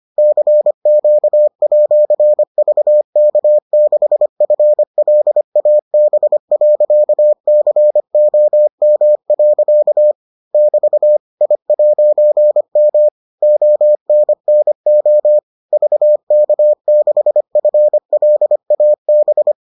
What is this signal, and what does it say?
Morse code, heard as a keyed tone, about 25 wpm, 600 Hz. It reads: CQ@VK6FLAB.COM. <BT> I'M ONNO VK6FLAB